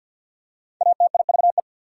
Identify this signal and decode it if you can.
Morse code, heard as a keyed tone, about 54 wpm, 700 Hz. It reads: ATIVE
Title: Prefixes and Suffixes 50wpm